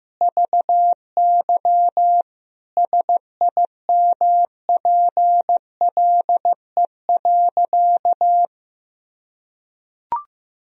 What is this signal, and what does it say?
Morse code, heard as a keyed tone, about 14 wpm, 700 Hz. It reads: VY SIMPLE.
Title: Sentences from Top 500 Words and Abbreviations 15wpm